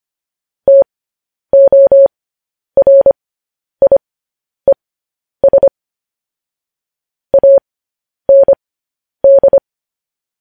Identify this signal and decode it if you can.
Morse code, heard as a keyed tone, about 25 wpm, spaced as for 10 wpm, 550 Hz. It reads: TORIES AND
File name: podcast / News headlines-1x-25x10